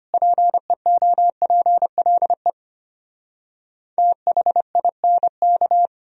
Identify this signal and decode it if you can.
Morse code, heard as a keyed tone, about 33 wpm, 700 Hz. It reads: PEOPLE THINK